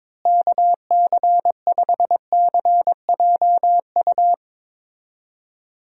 Morse code, heard as a keyed tone, about 21 wpm, 700 Hz. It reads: KC5CJU
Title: USA Call Signs - 6 Characters - Repeated 3x - 22wpm